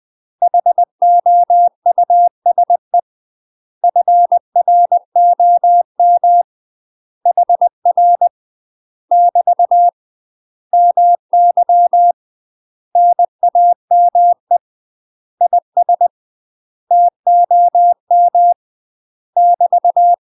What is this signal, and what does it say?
Morse code, heard as a keyed tone, about 20 wpm, 700 Hz. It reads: HOUSE FROM HR <BT> MY NAME IS TOM <BT>
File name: UK MASTER QSOs TRAINING / 20WPM / UK MASTER QSOs 0006